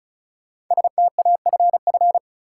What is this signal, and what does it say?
Morse code, heard as a keyed tone, about 36 wpm, 700 Hz. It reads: STAFF
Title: Sets of 5 Words Encore - New Word Drill - 35wpm